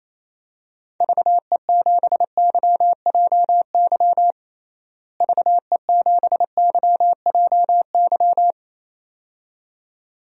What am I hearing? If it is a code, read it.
Morse code, heard as a keyed tone, about 28 wpm, 700 Hz. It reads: VE7YJY VE7YJY
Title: DXCC-100 Call Signs - 6 Characters - Repeated 2x - 28wpm